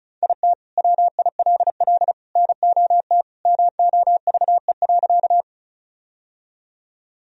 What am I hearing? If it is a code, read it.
Morse code, heard as a keyed tone, about 36 wpm, 700 Hz. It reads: IT WILL NOT MOVE.
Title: Sentences from Top 1000 Words - Review Part 1 - 35wpm